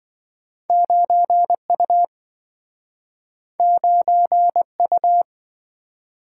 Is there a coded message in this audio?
Yes: Morse code from 0.7 to 5.2 s, heard as a keyed tone, about 27 wpm, 700 Hz. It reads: 9 U 9 U